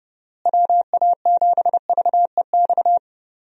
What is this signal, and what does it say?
Morse code, heard as a keyed tone, about 30 wpm, 700 Hz. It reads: WA7VEX
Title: USA Call Signs - 6 Characters - Repeated 2x - 30wpm